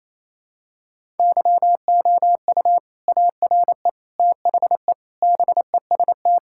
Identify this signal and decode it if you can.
Morse code, heard as a keyed tone, about 27 wpm, 700 Hz. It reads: YOU ARE THE BEST
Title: Sentences from Top 1000 Words - Review Part 6 - 28wpm